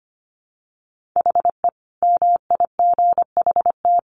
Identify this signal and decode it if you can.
Morse code, heard as a keyed tone, about 25 wpm, 700 Hz. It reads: HE MIGHT
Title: Sentences from Top 1400 Words 25wpm